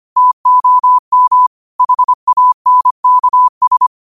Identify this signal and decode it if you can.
Morse code, heard as a keyed tone, about 25 wpm, 1000 Hz. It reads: TOM HANKS
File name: qotdfiles / 25wpm